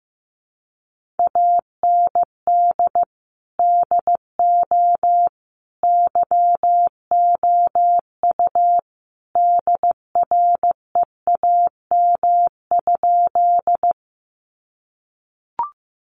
Morse code, heard as a keyed tone, about 15 wpm, 700 Hz. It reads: AND DO YOU DREAM?
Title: Sentences from Top 1500 Words 15wpm